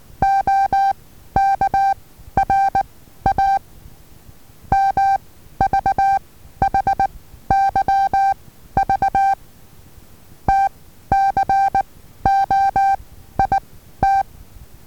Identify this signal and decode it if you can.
Morse code, heard as a keyed tone, about 19 wpm, 780 Hz. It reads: O K R A M V H Y V T C O I T
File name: Taviro HA6VA B